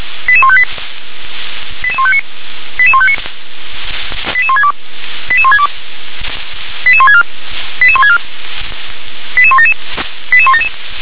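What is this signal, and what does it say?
Five-tone selective call (ZVEI), heard as quick runs of five tones, tones of about 70 ms. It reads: 88158 88159 88159 88162 88162 88164 88164 88177 88177
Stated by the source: Five-tone ZVEI selective call 88020, received by SDR.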